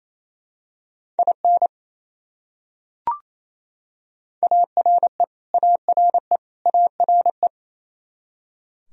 Morse code, heard as a keyed tone, about 28 wpm, 700 Hz. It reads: IN ARE ARE ARE
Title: Top 100 Words Repeated 28wpm